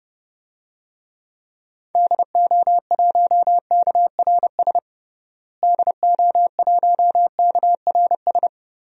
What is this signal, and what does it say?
Morse code, heard as a keyed tone, about 29 wpm, 700 Hz. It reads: DO1KRS DO1KRS
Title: DXCC-100 Call Signs - 6 Characters - Repeated 3x - 30wpm